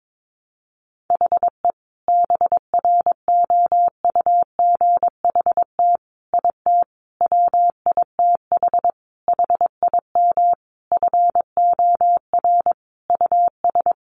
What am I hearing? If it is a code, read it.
Morse code, heard as a keyed tone, about 22 wpm, 700 Hz. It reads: HE BROUGHT IT WITH HIM FOR US
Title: Sentences from Top 500 Words 22wpm